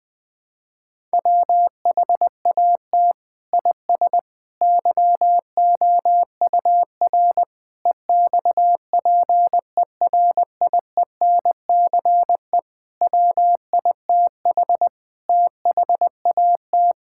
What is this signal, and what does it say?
Morse code, heard as a keyed tone, about 20 wpm, 700 Hz. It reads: WHAT IS YOUR EXPERIENCE WITH THAT
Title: Sentences from Top 1000 Words - Review Part 5 - 20wpm